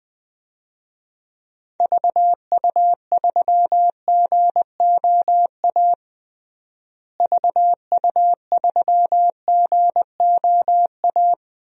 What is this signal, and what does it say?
Morse code, heard as a keyed tone, about 20 wpm, 700 Hz. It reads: VU3GOA VU3GOA
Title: DXCC-100 Call Signs - 6 Characters - Repeated 2x - 20wpm